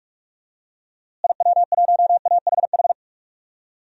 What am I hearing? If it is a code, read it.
Morse code, heard as a keyed tone, about 46 wpm, 700 Hz. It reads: IW1AHH
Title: DXCC-100 Call Signs - 6 Characters - Repeated 2x - 45wpm